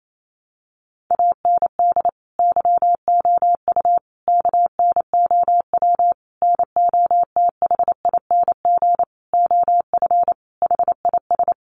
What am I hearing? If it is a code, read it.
Morse code, heard as a keyed tone, about 28 wpm, 700 Hz. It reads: AND YOU KNOW NOTHING OF HIS
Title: Sentences from Top 1200 Words 28wpm